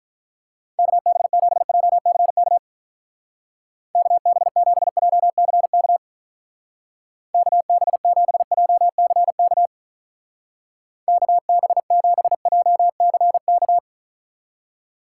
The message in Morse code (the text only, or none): KB7JCK KB7JCK KB7JCK KB7JCK